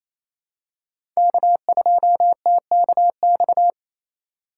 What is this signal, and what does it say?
Morse code, heard as a keyed tone, about 27 wpm, 700 Hz. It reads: K2TKX
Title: USA Call Signs - 2x Character Spacing - 28wpm